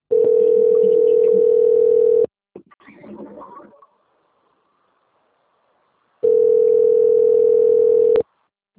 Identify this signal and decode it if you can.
Ringback tone (North America), 440 + 480 Hz.